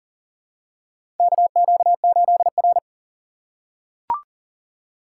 Morse code, heard as a keyed tone, about 40 wpm, 700 Hz. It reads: KQ8R
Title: USA Call Signs - 4 Characters - 4x Character Spacing - 40wpm